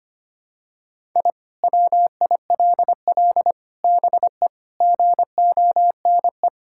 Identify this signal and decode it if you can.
Morse code, heard as a keyed tone, about 25 wpm, 700 Hz. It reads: I WILL BE GONE
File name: practice / 25wpm / Sentences with 4-Word Phrases - 25wpm